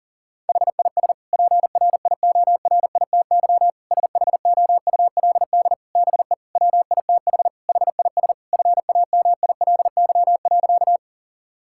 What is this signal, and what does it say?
Morse code, heard as a keyed tone, about 37 wpm, 700 Hz. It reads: HIS PRIORITY SHOULD BE WITH HIS FAMILY.
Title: Sentences from Top 1600 Words 40wpm